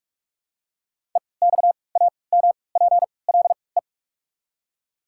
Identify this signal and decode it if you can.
Morse code, heard as a keyed tone, about 42 wpm, spaced as for 25 wpm, 700 Hz. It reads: EXAMPLE
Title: 3x Character Spacing - Top 300 Words - 45wpm